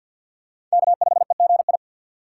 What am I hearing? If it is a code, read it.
Morse code, heard as a keyed tone, about 51 wpm, 700 Hz. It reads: K5EGI
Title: USA Call Signs - 5 Characters - 2x Character Spacing - 50wpm